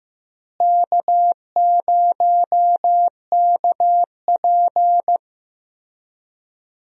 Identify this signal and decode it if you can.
Morse code, heard as a keyed tone, about 15 wpm, 700 Hz. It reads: K0KP
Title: USA Call Signs - 4 Characters - Repeated 3x - 15wpm